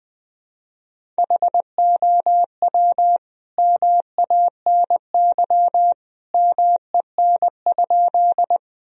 Morse code, heard as a keyed tone, about 20 wpm, 700 Hz. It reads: HOW MANY MEN?
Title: Sentences from Top 1000 Words - Review Part 1 - 20wpm